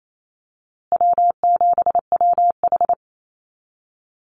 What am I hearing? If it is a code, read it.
Morse code, heard as a keyed tone, about 28 wpm, 700 Hz. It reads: W7WH
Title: USA Call Signs Repeated 2x 28wpm